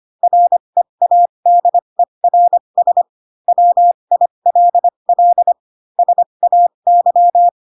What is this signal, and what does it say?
Morse code, heard as a keyed tone, about 25 wpm, spaced as for 22 wpm, 700 Hz. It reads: READERS WILL SAY